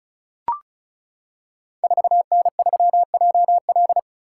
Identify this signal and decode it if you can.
Morse code, heard as a keyed tone, about 36 wpm, 700 Hz. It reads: 4N3JL
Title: Code Groups - 5 Letter:Numbers - 35wpm